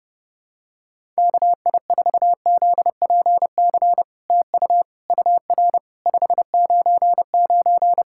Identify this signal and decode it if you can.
Morse code, heard as a keyed tone, about 30 wpm, 700 Hz. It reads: KI4ZPC TU UR 599
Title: SOTA QSO Practice 30wpm